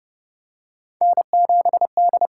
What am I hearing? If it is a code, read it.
Morse code, heard as a keyed tone, about 30 wpm, 700 Hz. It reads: N7D
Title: USA Call Signs - 5 Characters - 6x Character Spacing - 30wpm